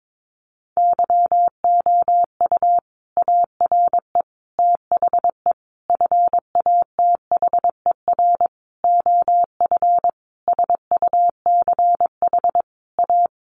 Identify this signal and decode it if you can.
Morse code, heard as a keyed tone, about 22 wpm, 700 Hz. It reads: YOU ARE THE FATHER OF SUCH A
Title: Sentences from Top 1500 Words 22wpm